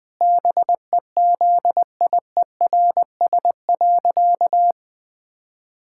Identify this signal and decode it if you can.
Morse code, heard as a keyed tone, about 20 wpm, 700 Hz. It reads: BEZIERS.
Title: Pangram Challenge v2 - 20wpm